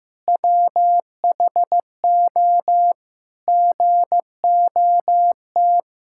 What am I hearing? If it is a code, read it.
Morse code, heard as a keyed tone, about 16 wpm, 700 Hz. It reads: WHO GOT
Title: Sentences from Top 1600 Words 15wpm